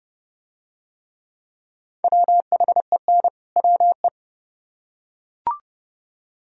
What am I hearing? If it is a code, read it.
Morse code, heard as a keyed tone, about 29 wpm, 700 Hz. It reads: WHEN WE